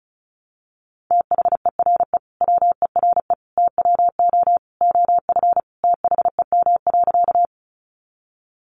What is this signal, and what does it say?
Morse code, heard as a keyed tone, about 34 wpm, 700 Hz. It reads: THERE WERE TWO OF THEM.
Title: Sentences from Top 100 Words 35wpm